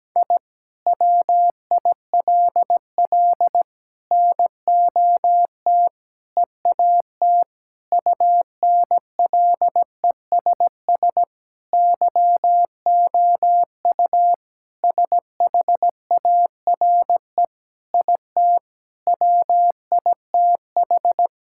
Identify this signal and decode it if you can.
Morse code, heard as a keyed tone, about 17 wpm, 700 Hz. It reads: I WILL NOT EAT UNLESS YOU SHARE IT WITH
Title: Sentences from Top 1000 Words - Review Part 6 - 17wpm